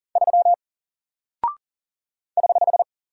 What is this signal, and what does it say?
Morse code, heard as a keyed tone, about 40 wpm, 700 Hz. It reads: 3 <HH>
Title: Single Character (Letter:Number:Punctuation:Prosign) Alt 40wpm